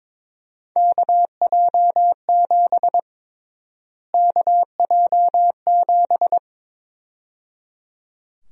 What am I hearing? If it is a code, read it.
Morse code, heard as a keyed tone, about 22 wpm, 700 Hz. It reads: KJ7 KJ7
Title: USA Call Sign Prefixes - Repeated 2x - 22wpm